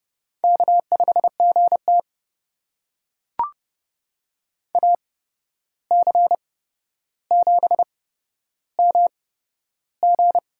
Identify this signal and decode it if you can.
Morse code, heard as a keyed tone, about 30 wpm, spaced as for 12 wpm, 700 Hz. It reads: K5GT AC7MG
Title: USA Call Signs - 8x Character Spacing - 30wpm